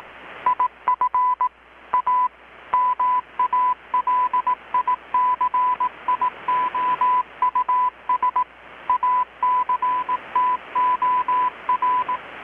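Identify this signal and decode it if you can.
Morse code, heard as a keyed tone, about 18 wpm, 1000 Hz. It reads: IF A MALICIOUS ACTOR
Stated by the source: Morse from KPH's Night of Nights XX maritime radiotelegraphy event, received in Chiapas, Mexico about 3,800 km away on a Kenwood R-600 with a grounded 10 m long-wire antenna.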